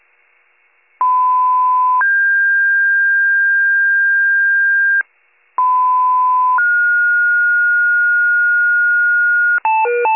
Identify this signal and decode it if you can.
Two-tone page, heard as a one-second tone, then a three-second tone, about 1010 then 1600 Hz, about 1010 then 1430 Hz, with noise. It is heard from 1.0 to 9.6 s.